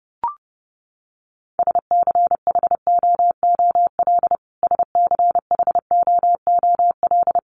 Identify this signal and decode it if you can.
Morse code, heard as a keyed tone, about 30 wpm, 700 Hz. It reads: SCHOOL SCHOOL
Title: Top 500 Words Repeated 30wpm